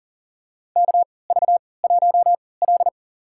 Morse code, heard as a keyed tone, about 40 wpm, spaced as for 23 wpm, 700 Hz. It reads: KV1L
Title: USA Call Signs - 3x Character Spacing - 40wpm